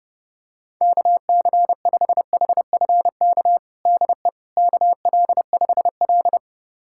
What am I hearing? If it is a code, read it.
Morse code, heard as a keyed tone, about 30 wpm, 700 Hz. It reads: KC5HFK DE KL5L
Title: USA Callsign DE Callsign 30wpm